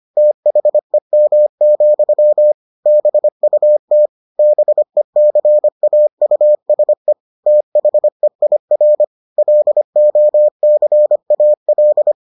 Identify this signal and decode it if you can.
Morse code, heard as a keyed tone, about 25 wpm, 600 Hz. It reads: THEM, BUT BECAUSE THEIR LOCAL